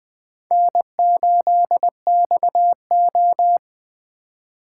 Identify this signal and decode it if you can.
Morse code, heard as a keyed tone, about 21 wpm, 700 Hz. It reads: N8XO